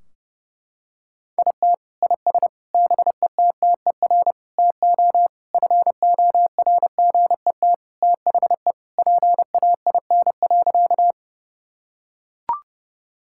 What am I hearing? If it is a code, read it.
Morse code, heard as a keyed tone, about 32 wpm, 700 Hz. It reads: IT IS BETTER TO FORGET THE PAIN.
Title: Sentences from Top 1600 Words 30wpm